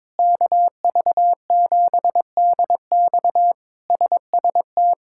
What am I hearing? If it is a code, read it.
Morse code, heard as a keyed tone, about 22 wpm, 700 Hz. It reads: KV7DX SST
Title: K1USN SST QSO Practice - 22wpm